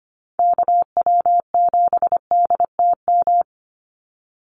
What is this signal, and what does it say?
Morse code, heard as a keyed tone, about 25 wpm, 700 Hz. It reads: KW7DTM
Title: USA Call Signs 25wpm